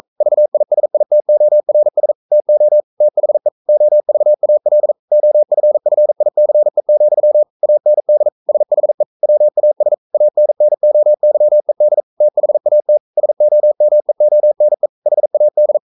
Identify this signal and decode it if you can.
Morse code, heard as a keyed tone, about 42 wpm, 600 Hz. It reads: VISITORS TO THE OVAL OFFICE, AND SHE WAS ANNOYED THAT SOMEONE HAD
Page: Romanen Never af Ken Follet på engelsk.